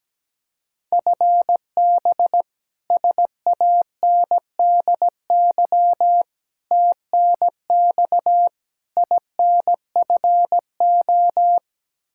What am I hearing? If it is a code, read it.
Morse code, heard as a keyed tone, about 17 wpm, 700 Hz. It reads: FB SANDY TNX INFO
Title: Practice QSO - Exchange 3 - More Info - 17wpm